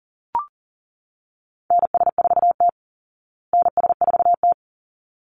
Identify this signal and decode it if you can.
Morse code, heard as a keyed tone, about 40 wpm, 700 Hz. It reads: NS4T NS4T
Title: Call Signs Repeated 3x - 40wpm